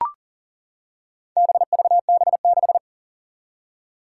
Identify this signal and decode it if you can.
Morse code, heard as a keyed tone, about 39 wpm, 700 Hz. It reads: BVB6
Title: Four Challenging Letter-Number Combinations 40wpm